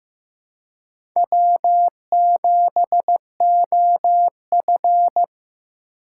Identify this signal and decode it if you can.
Morse code, heard as a keyed tone, about 16 wpm, 700 Hz. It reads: W7OF
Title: USA Call Signs - 4 Characters - 2x Character Spacing - 15wpm